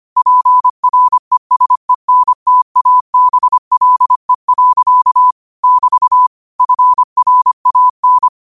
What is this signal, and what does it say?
Morse code, heard as a keyed tone, about 25 wpm, 1000 Hz. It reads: PRESENTABLE. <BT> FRAN
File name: qotdfiles / 25wpm